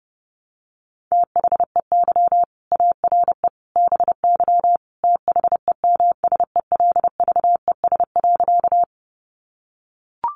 Morse code, heard as a keyed tone, about 30 wpm, 700 Hz. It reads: THEY ARE BY THEMSELVES.
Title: Sentences from Top 800 Words 30wpm